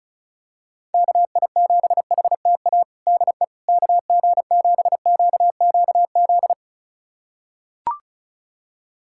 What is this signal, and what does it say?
Morse code, heard as a keyed tone, about 34 wpm, 700 Hz. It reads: KI7HTA DE KG7QQZ
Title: USA Callsign DE Callsign - 6 Characters - 35wpm